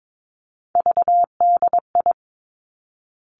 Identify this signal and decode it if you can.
Morse code, heard as a keyed tone, about 22 wpm, 700 Hz. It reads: VDI
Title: Three Challenging Letter-Number Combinations 22wpm